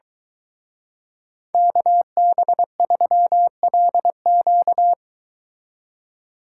Morse code, heard as a keyed tone, about 22 wpm, 700 Hz. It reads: KB3LQ